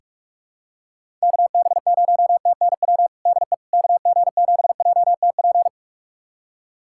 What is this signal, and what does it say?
Morse code, heard as a keyed tone, about 41 wpm, 700 Hz. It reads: KB0TNW DE KG7JTP
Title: USA Callsign DE Callsign - 6 Characters - 45wpm